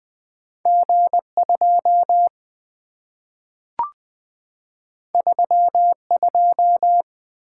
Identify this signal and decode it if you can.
Morse code, heard as a keyed tone, about 20 wpm, 700 Hz. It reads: G2 32
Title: Two Letter-Number Combinations - 20wpm